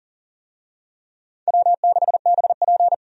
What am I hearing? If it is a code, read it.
Morse code, heard as a keyed tone, about 40 wpm, 700 Hz. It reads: W6BP
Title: USA Call Signs - 4 Characters - 40wpm